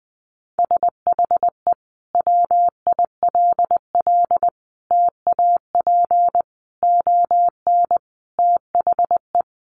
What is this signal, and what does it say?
Morse code, heard as a keyed tone, about 20 wpm, 700 Hz. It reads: SHE WILL TAP ON THE
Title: Sentences from Top 1400 Words 20wpm